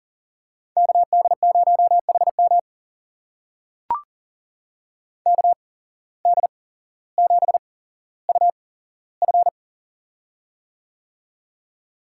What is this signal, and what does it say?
Morse code, heard as a keyed tone, about 40 wpm, spaced as for 17 wpm, 700 Hz. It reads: KD0HM KD7UF
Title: USA Call Signs - 5 Characters - 8x Character Spacing - 40wpm